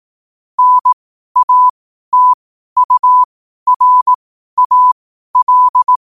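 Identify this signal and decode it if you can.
Morse code, heard as a keyed tone, about 18 wpm, spaced as for 12 wpm, 1000 Hz. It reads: NATURAL
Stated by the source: Morse practice sent at 12 wpm.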